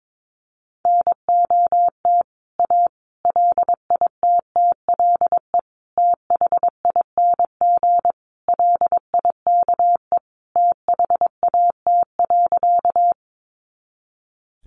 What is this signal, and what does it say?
Morse code, heard as a keyed tone, about 22 wpm, 700 Hz. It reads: NOT A LITTLE THING LIKE THAT.
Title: A Sound Of Thunder 22wpm